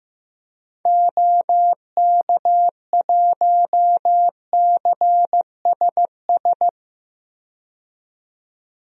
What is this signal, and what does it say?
Morse code, heard as a keyed tone, about 16 wpm, 700 Hz. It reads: OK1CSS
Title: DXCC-100 Call Signs - 6 Characters - 15wpm